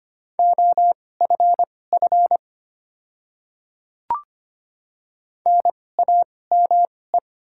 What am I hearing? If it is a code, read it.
Morse code, heard as a keyed tone, about 25 wpm, spaced as for 18 wpm, 700 Hz. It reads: OFF NAME
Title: 2x Character Spacing - Top 200 Words - 25wpm